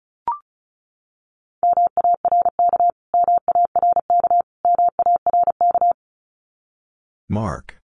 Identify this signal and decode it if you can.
Morse code, heard as a keyed tone, about 34 wpm, 700 Hz. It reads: MARK MARK MARK
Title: Top 500 Words Repeated 35wpm